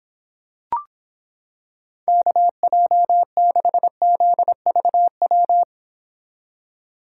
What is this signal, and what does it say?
Morse code, heard as a keyed tone, about 26 wpm, 700 Hz. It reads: KJ6ZVW